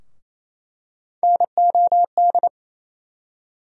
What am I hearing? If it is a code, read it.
Morse code, heard as a keyed tone, about 28 wpm, 700 Hz. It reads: NOD
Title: USA Call Sign Postfixes - 3 Letters 28wpm